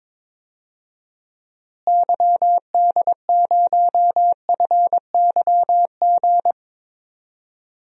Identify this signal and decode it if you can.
Morse code, heard as a keyed tone, about 22 wpm, 700 Hz. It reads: YD0FYG
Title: DXCC-100 Call Signs - 6 Characters - Repeated 2x - 22wpm